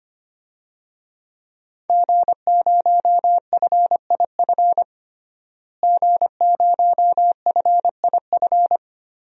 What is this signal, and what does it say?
Morse code, heard as a keyed tone, about 25 wpm, 700 Hz. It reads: G0FIF G0FIF
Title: DXCC-100 Call Signs - 5 Characters - Repeated 3x - 25wpm